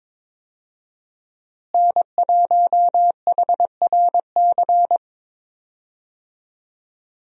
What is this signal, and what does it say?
Morse code, heard as a keyed tone, about 21 wpm, 700 Hz. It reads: N1HRC
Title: USA Call Signs - 5 Characters - Repeated 3x - 22wpm